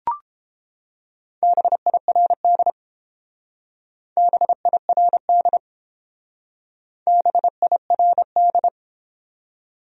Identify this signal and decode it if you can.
Morse code, heard as a keyed tone, about 30 wpm, 700 Hz. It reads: BIRD BIRD BIRD